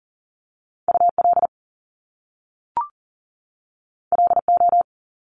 Morse code, heard as a keyed tone, about 41 wpm, 700 Hz. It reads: UL LO
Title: Top Two Letter Combinations in Words 40wpm